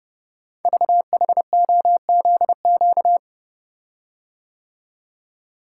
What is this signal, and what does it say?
Morse code, heard as a keyed tone, about 30 wpm, 690 Hz. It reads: VHOZQ
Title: Code Groups - 5 Letters - 30wpm